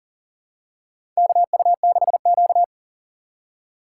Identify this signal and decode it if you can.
Morse code, heard as a keyed tone, about 40 wpm, 700 Hz. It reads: KU6Q
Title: USA Call Signs - 4 Characters - 3x Character Spacing - 40wpm